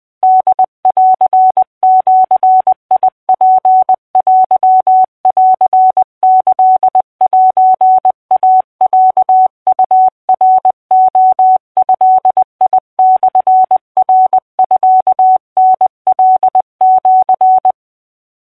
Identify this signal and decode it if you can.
Morse code (Japanese Wabun code), heard as a keyed tone, about 20 wpm, 750 Hz. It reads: ホンジツテンキセイロウナレドモナミタカシ
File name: 20wpm